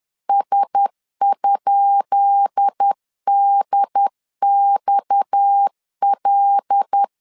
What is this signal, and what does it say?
Morse code, heard as a keyed tone, about 10 wpm, 800 Hz. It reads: S?DXL
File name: LEZIONI / 10 SIGNS / 10B INT 10wpm 2844